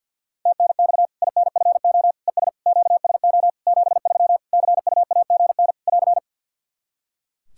Morse code, heard as a keyed tone, about 53 wpm, 700 Hz. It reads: TNX INFO ES QSO 73 CUAGN <AR>
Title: Practice QSO - Exchange 4 - Ending - 50wpm